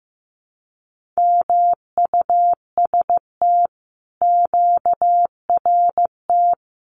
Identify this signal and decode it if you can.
Morse code, heard as a keyed tone, about 15 wpm, 700 Hz. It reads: MUST QRT
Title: Common QSO Phrases Encore 15wpm